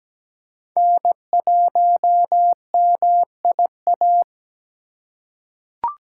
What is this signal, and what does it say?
Morse code, heard as a keyed tone, about 17 wpm, 700 Hz. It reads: N1MIA
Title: USA Call Signs - 5 Characters - 4x Character Spacing - 17wpm